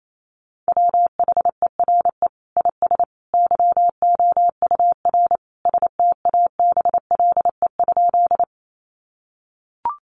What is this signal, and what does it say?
Morse code, heard as a keyed tone, about 28 wpm, 700 Hz. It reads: WHERE IS YOUR STABLE?
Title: Sentences from Top 1500 Words 28wpm